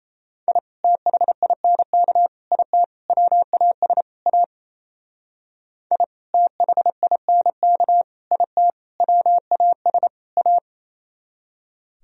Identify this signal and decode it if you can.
Morse code, heard as a keyed tone, about 34 wpm, 700 Hz. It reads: I THINK IT WAS A I THINK IT WAS A
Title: Speed-Racing - Sets of 5 Words 28wpm